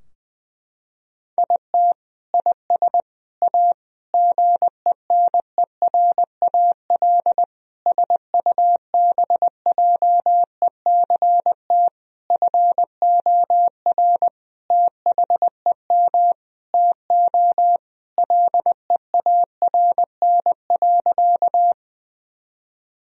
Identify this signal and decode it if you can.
Morse code, heard as a keyed tone, about 21 wpm, 700 Hz. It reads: IT IS A GENERAL SUBJECT FOR THEM TO LEARN.
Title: Sentences from Top 1000 Words - Review Part 3 - 20wpm